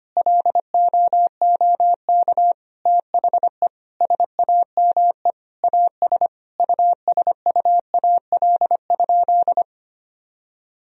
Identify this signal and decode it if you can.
Morse code, heard as a keyed tone, about 25 wpm, 700 Hz. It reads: LOOK THE SAME AS USUAL?
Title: Sentences from Top 1000 Words - Review Part 2 - 25wpm